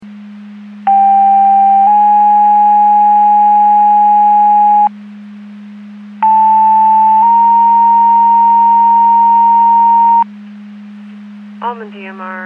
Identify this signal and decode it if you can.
Two-tone page, heard as a one-second tone, then a three-second tone, about 790 then 830 Hz, about 880 then 930 Hz. It is heard from 0.9 to 10.2 s.
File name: Portage 154.385 Paging